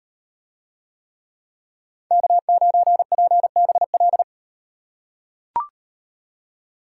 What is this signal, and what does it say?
Morse code, heard as a keyed tone, about 38 wpm, 700 Hz. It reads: K9PBL